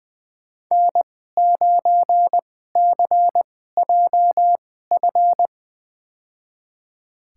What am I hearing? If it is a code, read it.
Morse code, heard as a keyed tone, about 20 wpm, spaced as for 14 wpm, 700 Hz. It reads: N9CJF
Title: USA Call Signs - 5 Characters - 2x Character Spacing - 20wpm